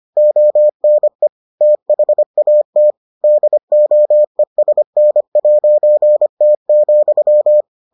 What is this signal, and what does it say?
Morse code, heard as a keyed tone, about 25 wpm, 600 Hz. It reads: ONE THAT DOESN'T,